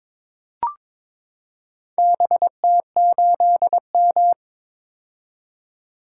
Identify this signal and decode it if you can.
Morse code, heard as a keyed tone, about 23 wpm, 700 Hz. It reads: BT8M
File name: practice / 22wpm / Four Letter-Number Combinations - 22wpm